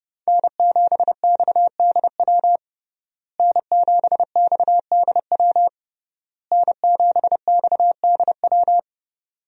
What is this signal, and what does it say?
Morse code, heard as a keyed tone, about 31 wpm, 700 Hz. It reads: N7XDW N7XDW N7XDW
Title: USA Call Signs - 5 Characters - Repeated 3x - 30wpm